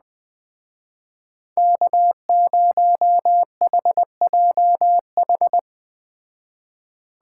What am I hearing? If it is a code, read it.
Morse code, heard as a keyed tone, about 19 wpm, 700 Hz. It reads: K0HJH
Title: Speed Racing - USA 1x3 Call Signs 17wpm